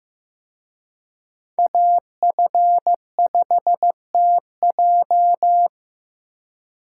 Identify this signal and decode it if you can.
Morse code, heard as a keyed tone, about 18 wpm, 700 Hz. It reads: AF5TJ
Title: Speed Racing - USA 2x2 Call Signs 15wpm